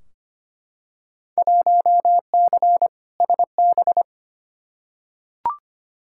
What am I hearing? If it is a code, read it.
Morse code, heard as a keyed tone, about 25 wpm, 700 Hz. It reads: 1C SB
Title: Field Day Exchanges Encore - 25wpm